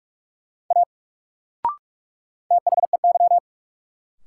Morse code, heard as a keyed tone, about 46 wpm, 700 Hz. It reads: A THEY
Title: Sentences from Top 100 Words - New Word Drill - 45wpm